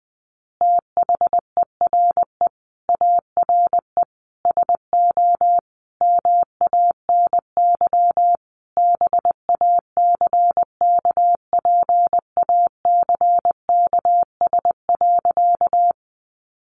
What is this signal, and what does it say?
Morse code, heard as a keyed tone, about 20 wpm, 700 Hz. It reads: THERE ARE SO MANY BACKPACKS.
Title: Additional Easy Phrases 20wpm